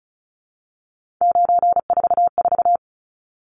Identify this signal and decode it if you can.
Morse code, heard as a keyed tone, about 34 wpm, 700 Hz. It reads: 944
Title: Numbers - 3-Digit 35wpm